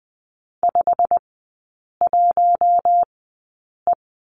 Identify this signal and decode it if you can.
Morse code, heard as a keyed tone, about 20 wpm, spaced as for 8 wpm, 700 Hz. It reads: 51E
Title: 087 Lesson 15e - Number 5 - Three Characters 20wpm